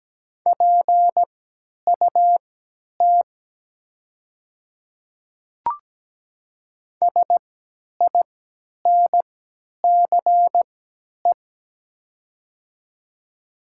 Morse code, heard as a keyed tone, about 17 wpm, spaced as for 10 wpm, 700 Hz. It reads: PUT SINCE
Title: 3x Character Spacing - Top 300 Words - 17wpm